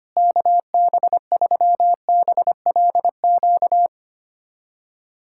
Morse code, heard as a keyed tone, about 25 wpm, 700 Hz. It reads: KB3BLQ
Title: USA Callsign DE Callsign - 6 Characters - 25wpm